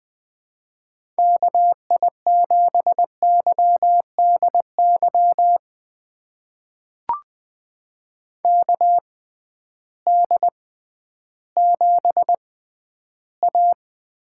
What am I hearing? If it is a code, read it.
Morse code, heard as a keyed tone, about 21 wpm, spaced as for 12 wpm, 700 Hz. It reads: KI7YDY KD7A
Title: USA Call Signs - 6 Characters - 6x Character Spacing - 20wpm